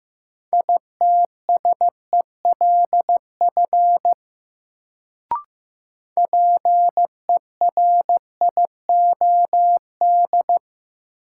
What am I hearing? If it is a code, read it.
Morse code, heard as a keyed tone, about 15 wpm, 700 Hz. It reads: ITSELF PERIOD
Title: Sentences from Top 800 Words - New Word Drill - 15wpm